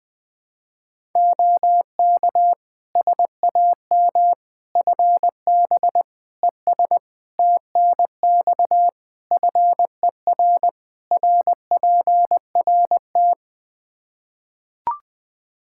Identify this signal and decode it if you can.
Morse code, heard as a keyed tone, about 20 wpm, 700 Hz. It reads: OK SAM FB ES TNX FER RPRT
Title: Practice QSO - Exchange 2 - Info - 20wpm